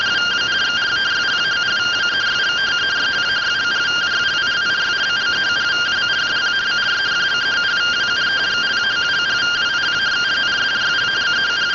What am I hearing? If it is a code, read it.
Two-tone teleprinter signal, about 170 Hz shift.
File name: findings / CT 1 Scout 0002.gen-35 rtty rx